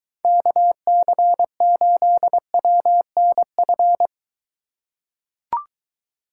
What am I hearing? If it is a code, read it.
Morse code, heard as a keyed tone, about 22 wpm, 700 Hz. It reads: KC8WNF